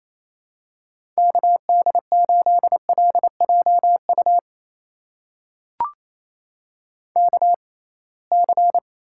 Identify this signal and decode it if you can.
Morse code, heard as a keyed tone, about 27 wpm, spaced as for 21 wpm, 700 Hz. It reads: KD8LJU KC
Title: USA Call Signs - 6x Character Spacing - 28wpm